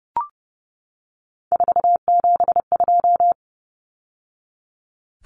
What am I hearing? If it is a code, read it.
Morse code, heard as a keyed tone, about 30 wpm, 700 Hz. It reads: 472
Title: Numbers - 3-Digit 30wpm